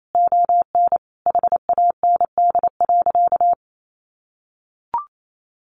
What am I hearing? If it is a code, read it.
Morse code, heard as a keyed tone, about 28 wpm, 700 Hz. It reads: ON HAND.
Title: Sentences from Top 1500 Words 28wpm